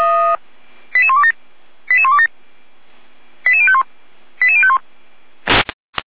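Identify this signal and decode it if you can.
Five-tone selective call (ZVEI), heard as quick runs of five tones, tones of about 70 ms. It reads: 88127 88127 88051 88051
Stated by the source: ZVEI five-tone selective call received on an SDR, recorded by fire brigade Feuerwehr Orlamünde.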